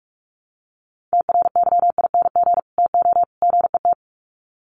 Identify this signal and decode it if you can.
Morse code, heard as a keyed tone, about 44 wpm, 700 Hz. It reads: TRYING TO GET
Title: Sets of 3 Words 45wpm